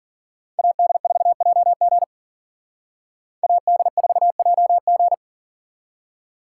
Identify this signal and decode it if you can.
Morse code, heard as a keyed tone, about 45 wpm, 700 Hz. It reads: AD4JG AD4JG
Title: Speed Racing - USA 2x2 Call Signs 40wpm